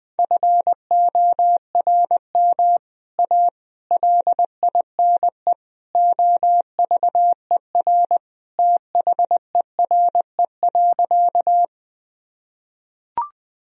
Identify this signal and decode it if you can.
Morse code, heard as a keyed tone, about 21 wpm, 700 Hz. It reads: FORM A LINE OVER THERE.
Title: Sentences from Top 1000 Words - Review Part 1 - 20wpm